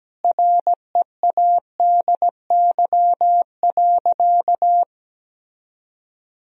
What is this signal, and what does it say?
Morse code, heard as a keyed tone, about 17 wpm, 700 Hz. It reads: READY.
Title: Sentences from Top 1000 Words - Review Part 1 - 17wpm